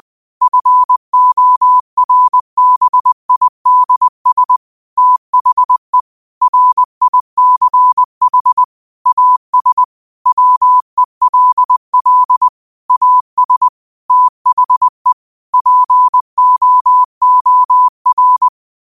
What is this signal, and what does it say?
Morse code, heard as a keyed tone, about 20 wpm, 1000 Hz. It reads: FORBIDS THE RICH AS WELL AS THE POOR